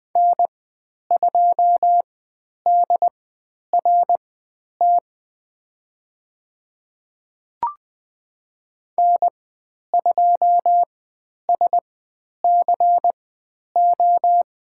Morse code, heard as a keyed tone, about 20 wpm, spaced as for 10 wpm, 700 Hz. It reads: N2DRT N2SCO
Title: 095 Lesson 16g - Letter c - Call Signs 20wpm